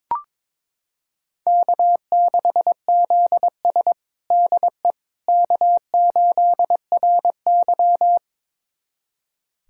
Morse code, heard as a keyed tone, about 22 wpm, 700 Hz. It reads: K6ZS DE K8RY
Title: USA Callsign DE Callsign - 4 Characters - 22wpm